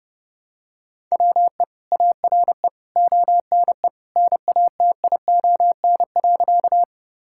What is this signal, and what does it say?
Morse code, heard as a keyed tone, about 32 wpm, 700 Hz. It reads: WE ARE ONE NATION.
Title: Sentences from Top 1000 Words - Review Part 6 - 30wpm